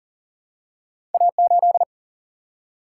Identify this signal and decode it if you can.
Morse code, heard as a keyed tone, about 40 wpm, 700 Hz. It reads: A 8
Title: Single-Letter Single-Number 40wpm